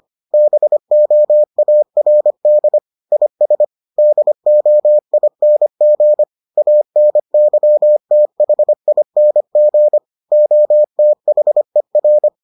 Morse code, heard as a keyed tone, about 25 wpm, 600 Hz. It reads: BOARD IS DOING ANYTHING OTHER